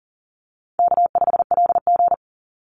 Morse code, heard as a keyed tone, about 40 wpm, 700 Hz. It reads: K5LG
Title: Call Signs - 40wpm